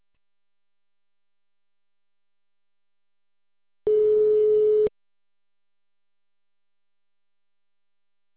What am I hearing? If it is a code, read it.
Ringback tone (Europe, 425 Hz).